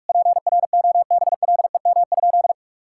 Morse code, heard as a keyed tone, about 41 wpm, 700 Hz. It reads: PROBLEM?
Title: Sentences from Top 1000 Words - Review Part 6 - 45wpm